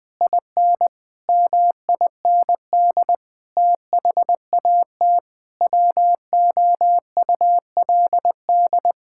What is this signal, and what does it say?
Morse code, heard as a keyed tone, about 20 wpm, 700 Hz. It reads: IN MIND THAT WOULD
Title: Sentences with 9-Word Phrases - 20wpm